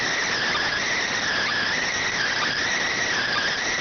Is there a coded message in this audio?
Yes: Emergency Alert System (SAME) tones from 0.7 to 3.5 s, heard as harsh buzzing bursts.